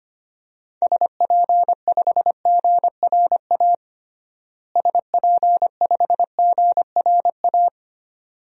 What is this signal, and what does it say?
Morse code, heard as a keyed tone, about 25 wpm, 700 Hz. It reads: SP5GRA SP5GRA
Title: DXCC-100 Call Signs - 6 Characters - Repeated 2x - 25wpm